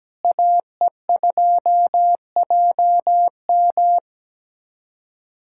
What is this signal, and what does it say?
Morse code, heard as a keyed tone, about 17 wpm, 700 Hz. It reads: AE2JM
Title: USA Callsign DE Callsign 17wpm